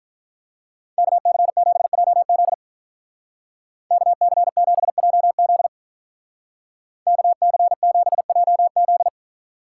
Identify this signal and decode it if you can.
Morse code, heard as a keyed tone, about 45 wpm, 700 Hz. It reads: KC7JZ KC7JZ KC7JZ